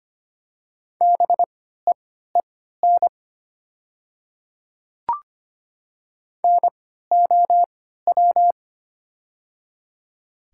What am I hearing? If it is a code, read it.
Morse code, heard as a keyed tone, about 25 wpm, spaced as for 14 wpm, 700 Hz. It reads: BEEN NOW
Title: 3x Character Spacing - Top 100 Words - 25wpm